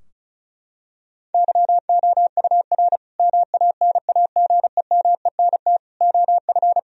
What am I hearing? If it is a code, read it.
Morse code, heard as a keyed tone, about 36 wpm, 700 Hz. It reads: YOUR MANAGEMENT OF
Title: Sentences from Top 1000 Words - Review Part 4 - 35wpm